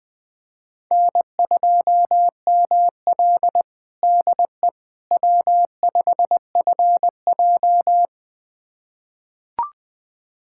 Morse code, heard as a keyed tone, about 20 wpm, 700 Hz. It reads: N2ML DE W5FJ
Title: USA Callsign DE Callsign - 4 Characters - 20wpm